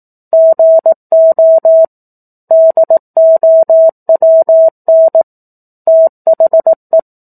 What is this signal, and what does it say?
Morse code, heard as a keyed tone, about 18 wpm, 650 Hz. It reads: GO DOWN THE
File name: 04-Chapter-04 18 WPM